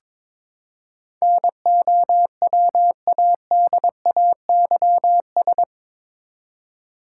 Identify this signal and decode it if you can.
Morse code, heard as a keyed tone, about 22 wpm, 700 Hz. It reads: NOWADAYS
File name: practice / 22wpm / Eight Letter Words - 22wpm